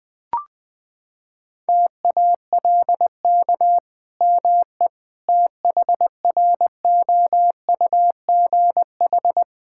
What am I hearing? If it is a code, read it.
Morse code, heard as a keyed tone, about 20 wpm, 700 Hz. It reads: TALK ME THROUGH
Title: Sentences from Top 1000 Words - Review Part 5 - 20wpm